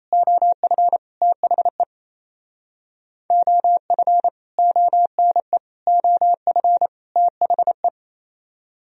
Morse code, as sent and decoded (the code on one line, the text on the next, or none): --- ..-. / - .... / --- ..-. / --- -. / --- ..-. / - .... .
OF THE OF ONE OF THE